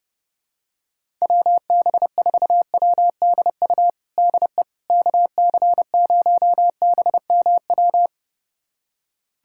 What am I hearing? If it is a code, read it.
Morse code, heard as a keyed tone, about 30 wpm, 700 Hz. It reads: WB4WDU DE KC0BMW
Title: USA Callsign DE Callsign - 6 Characters - 30wpm